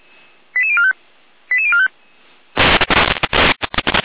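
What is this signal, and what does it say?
Five-tone selective call (ZVEI), heard as quick runs of five tones, tones of about 70 ms. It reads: 88045 88045
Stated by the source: ZVEI five-tone call 88247 received with an SDR, recorded on the Feuerwehr Orlamünde server.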